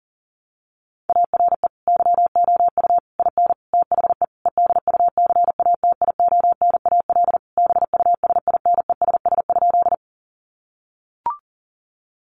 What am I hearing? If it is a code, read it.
Morse code, heard as a keyed tone, about 40 wpm, 700 Hz. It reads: ARE YOU IN THE EDUCATIONAL BUSINESS?
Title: Sentences from Top 1200 Words 40wpm